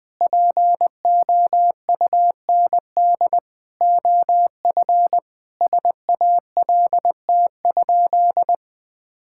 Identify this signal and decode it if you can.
Morse code, heard as a keyed tone, about 20 wpm, 700 Hz. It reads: POUND OF SALT?
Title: Sentences from Top 1000 Words - Review Part 5 - 20wpm